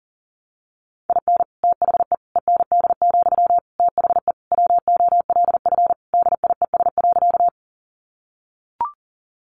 Morse code, heard as a keyed tone, about 40 wpm, 700 Hz. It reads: IN THE END, THE WOLF DIES.
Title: Additional Easy Phrases 40wpm